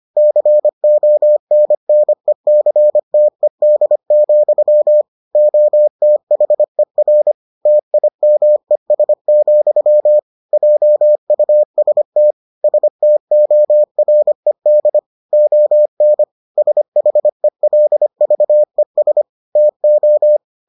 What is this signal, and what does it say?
Morse code, heard as a keyed tone, about 25 wpm, 600 Hz. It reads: CONNECTED, OTHER TIMES, JUST STORED ON SHELVES TO